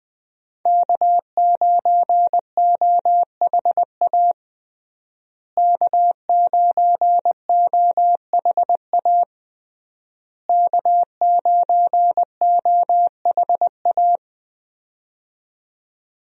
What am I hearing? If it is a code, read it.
Morse code, heard as a keyed tone, about 20 wpm, 700 Hz. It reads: K9OHA K9OHA K9OHA
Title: USA Call Signs - 5 Characters - Repeated 3x - 20wpm